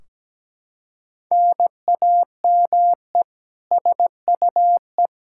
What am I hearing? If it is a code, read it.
Morse code, heard as a keyed tone, about 17 wpm, 700 Hz. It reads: NAME SUE
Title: Practice QSO - Exchange 1 - Beginning - 17wpm